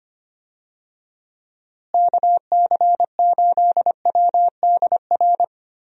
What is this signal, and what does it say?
Morse code, heard as a keyed tone, about 25 wpm, 700 Hz. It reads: KC8WDR
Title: USA Call Signs - 6 Characters - Repeated 2x - 25wpm